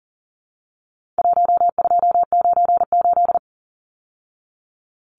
Morse code, heard as a keyed tone, about 40 wpm, 700 Hz. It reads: 1298
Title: Numbers - 4-Digit 40wpm